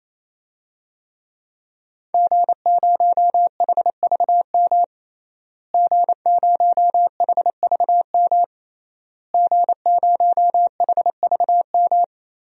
Morse code, heard as a keyed tone, about 28 wpm, 700 Hz. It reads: G0HVM G0HVM G0HVM
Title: DXCC-100 Call Signs - 5 Characters - Repeated 3x - 28wpm